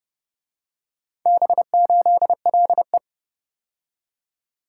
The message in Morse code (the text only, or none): B8LE